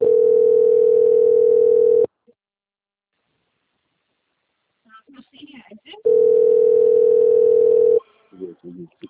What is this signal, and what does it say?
Ringback tone (North America), 440 + 480 Hz, with speech.
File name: out- - - - - .67915